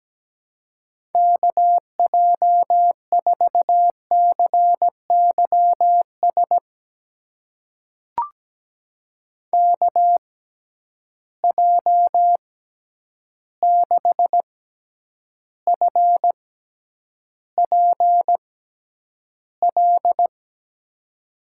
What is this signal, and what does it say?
Morse code, heard as a keyed tone, about 17 wpm, spaced as for 9 wpm, 700 Hz. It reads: KJ4CYS KJ6FPL
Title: USA Call Signs - 6 Characters - 6x Character Spacing - 17wpm